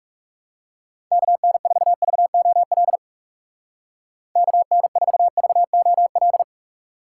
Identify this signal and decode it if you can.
Morse code, heard as a keyed tone, about 39 wpm, 700 Hz. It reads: KN4VOL KN4VOL